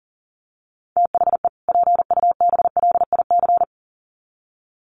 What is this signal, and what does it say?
Morse code, heard as a keyed tone, about 40 wpm, 700 Hz. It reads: THE PUBLIC
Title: Sets of 2 Words 40wpm